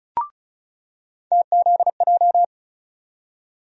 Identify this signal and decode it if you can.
Morse code, heard as a keyed tone, about 36 wpm, 700 Hz. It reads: TZJ
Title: USA Call Sign Postfixes - 3 Letters 35wpm